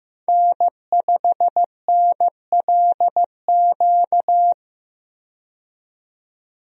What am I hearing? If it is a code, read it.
Morse code, heard as a keyed tone, about 15 wpm, 700 Hz. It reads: N5NLQ
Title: USA Call Signs - 5 Characters - 15wpm